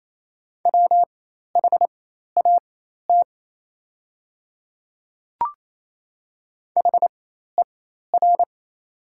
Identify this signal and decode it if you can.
Morse code, heard as a keyed tone, about 27 wpm, spaced as for 13 wpm, 700 Hz. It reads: WHAT HER
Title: 4x Character Spacing - Top 100 Words - 28wpm